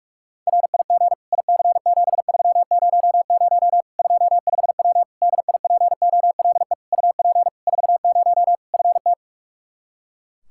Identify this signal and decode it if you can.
Morse code, heard as a keyed tone, about 41 wpm, 700 Hz. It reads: RIG IC7300 25W DIPOLE UP 40 FT
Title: K1USN SST QSO Practice 45wpm